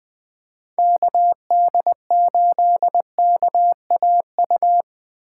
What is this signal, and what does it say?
Morse code, heard as a keyed tone, about 20 wpm, 700 Hz. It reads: KD8KAU
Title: Speed Racing - USA 2x3 Call Signs 17wpm